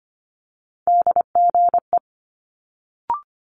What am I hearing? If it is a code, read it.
Morse code, heard as a keyed tone, about 24 wpm, 700 Hz. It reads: DGE
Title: Multi-letter Phonemes Quick Fire 25wpm